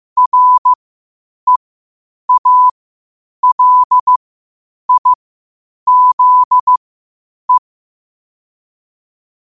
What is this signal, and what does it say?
Morse code, heard as a keyed tone, about 15 wpm, spaced as for 8 wpm, 1000 Hz. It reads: REALIZE
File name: qotdfiles / 8wpm